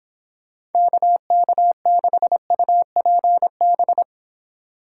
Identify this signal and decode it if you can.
Morse code, heard as a keyed tone, about 31 wpm, 700 Hz. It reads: KK6UPB